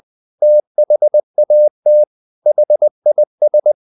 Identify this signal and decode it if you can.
Morse code, heard as a keyed tone, about 20 wpm, 600 Hz. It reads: THAT HIS
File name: the old man and the sea part 2 20wpm